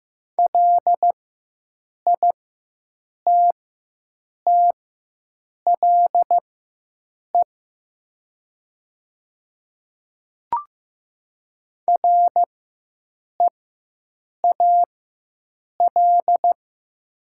Morse code, heard as a keyed tone, about 15 wpm, spaced as for 7 wpm, 700 Hz. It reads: LITTLE REAL
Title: 4x Character Spacing - Top 300 Words - 15wpm